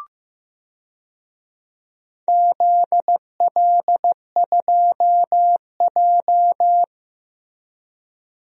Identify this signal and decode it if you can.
Morse code, heard as a keyed tone, about 15 wpm, 700 Hz. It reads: ZL2J
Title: DXCC-100 Call Signs - 4 Characters - Repeated 3x - 15wpm